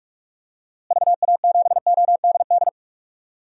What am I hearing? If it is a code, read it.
Morse code, heard as a keyed tone, about 46 wpm, 700 Hz. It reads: VA7ODD
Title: DXCC-100 Call Signs - 6 Characters - Repeated 3x - 45wpm